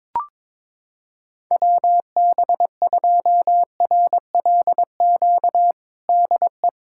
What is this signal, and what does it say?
Morse code, heard as a keyed tone, about 22 wpm, 700 Hz. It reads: WB2RLQ DE
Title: USA Callsign DE Callsign - 6 Characters - 22wpm